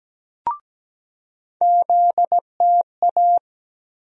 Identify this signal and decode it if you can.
Morse code, heard as a keyed tone, about 17 wpm, 700 Hz. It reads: ZTA
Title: USA Call Sign Postfixes - 3 Letters 17wpm